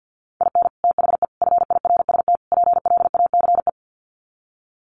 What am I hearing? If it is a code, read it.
Morse code, heard as a keyed tone, about 50 wpm, 700 Hz. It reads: IN THE FIRST PLACE
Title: Sets of 4 Words 50wpm